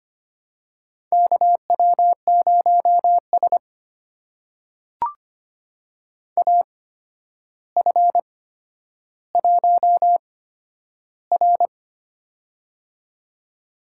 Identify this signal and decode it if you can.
Morse code, heard as a keyed tone, about 25 wpm, spaced as for 11 wpm, 700 Hz. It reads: KW0S AF1R
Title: USA Call Signs - 4 Characters - 8x Character Spacing - 25wpm